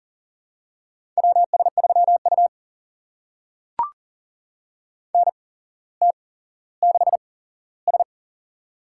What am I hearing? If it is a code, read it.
Morse code, heard as a keyed tone, about 42 wpm, spaced as for 17 wpm, 700 Hz. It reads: WS3U NT6S
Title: USA Call Signs - 4 Characters - 8x Character Spacing - 40wpm